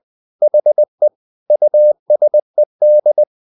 Morse code, heard as a keyed tone, about 20 wpm, 600 Hz. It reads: HE USED
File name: the old man and the sea part 2 20wpm